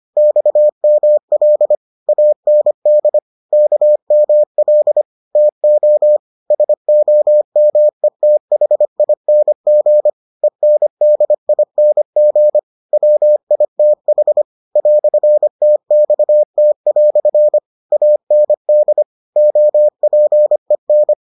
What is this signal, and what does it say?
Morse code, heard as a keyed tone, about 25 wpm, 600 Hz. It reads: XML AND KML TO SOMETHING ENDING WITH "TXT" AND OPEN